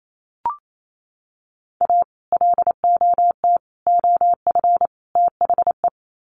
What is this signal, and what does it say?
Morse code, heard as a keyed tone, about 28 wpm, 700 Hz. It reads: A LOT OF THE
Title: Sets of 4 Words 28wpm